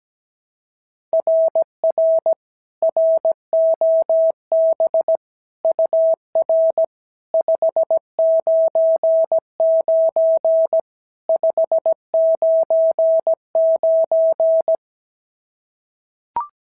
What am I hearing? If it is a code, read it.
Morse code, heard as a keyed tone, about 15 wpm, 650 Hz. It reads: RR ROB UR 599 599
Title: SOTA QSO Practice 17wpm